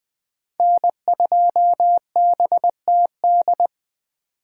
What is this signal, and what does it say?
Morse code, heard as a keyed tone, about 20 wpm, 700 Hz. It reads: N2BTD
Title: 2023 POTA Hunter Call Signs - 20wpm